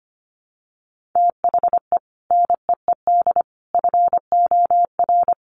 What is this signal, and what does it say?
Morse code, heard as a keyed tone, about 25 wpm, 700 Hz. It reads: THE NEED FOR
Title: Sets of 3 Words 25wpm